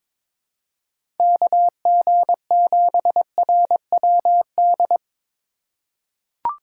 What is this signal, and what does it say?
Morse code, heard as a keyed tone, about 22 wpm, 700 Hz. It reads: KG7RWD
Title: USA Call Signs - 2x Character Spacing - 22wpm